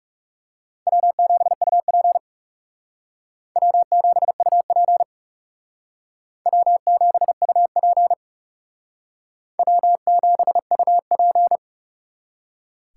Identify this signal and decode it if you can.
Morse code, heard as a keyed tone, about 36 wpm, 700 Hz. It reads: W7UP W7UP W7UP W7UP
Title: Speed Racing - USA 1x2 Call Signs 30wpm